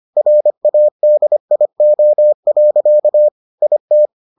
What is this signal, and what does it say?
Morse code, heard as a keyed tone, about 25 wpm, 600 Hz. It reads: RADIO. IT